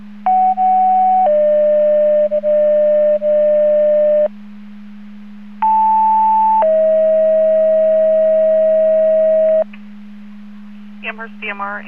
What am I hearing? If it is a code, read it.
Two-tone page, heard as a one-second tone, then a three-second tone, about 750 then 600 Hz, about 880 then 640 Hz, with speech. It is heard from 0.3 to 9.6 s.